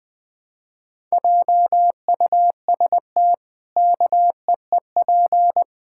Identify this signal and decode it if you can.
Morse code, heard as a keyed tone, about 21 wpm, 700 Hz. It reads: JUST KEEP
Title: Pangram Challenge v2 - 20wpm